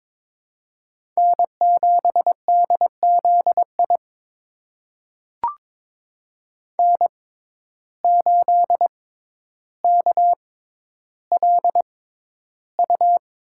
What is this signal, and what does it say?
Morse code, heard as a keyed tone, about 22 wpm, spaced as for 11 wpm, 700 Hz. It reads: N7DZI N8KLU
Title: USA Call Signs - 5 Characters - 6x Character Spacing - 22wpm